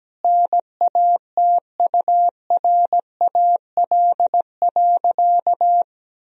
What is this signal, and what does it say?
Morse code, heard as a keyed tone, about 17 wpm, 700 Hz. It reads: NATURAL.
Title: Sentences from Top 1000 Words - Review Part 4 - 17wpm